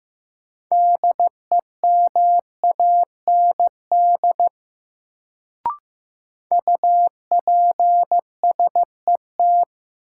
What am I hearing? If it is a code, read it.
Morse code, heard as a keyed tone, about 15 wpm, 700 Hz. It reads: DEMAND UPSET
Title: Sentences from Top 800 Words - New Word Drill - 15wpm